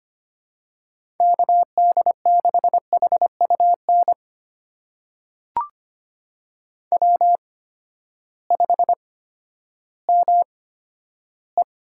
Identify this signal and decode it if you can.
Morse code, heard as a keyed tone, about 25 wpm, spaced as for 12 wpm, 700 Hz. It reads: KD6HUN W5ME
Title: USA Call Signs - 8x Character Spacing - 25wpm